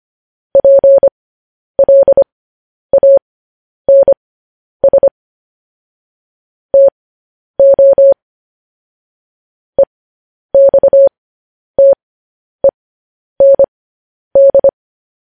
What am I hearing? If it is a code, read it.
Morse code, heard as a keyed tone, about 25 wpm, spaced as for 10 wpm, 550 Hz. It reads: PLANS TO EXTEND
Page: news headlines in morse code at 10 wpm